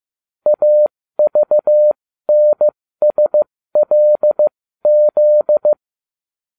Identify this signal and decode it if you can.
Morse code, heard as a keyed tone, about 15 wpm, spaced as for 13 wpm, 600 Hz. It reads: AVNSLZ